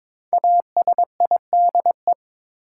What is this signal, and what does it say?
Morse code, heard as a keyed tone, about 22 wpm, 700 Hz. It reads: ASIDE
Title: Five Letter Words - 22wpm